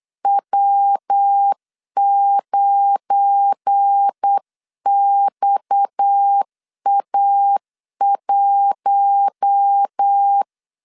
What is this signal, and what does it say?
Morse code, heard as a keyed tone, about 8 wpm, 800 Hz. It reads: W9XA1
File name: LEZIONI / 7 GF9Z / 7 GF9 8wpm 3545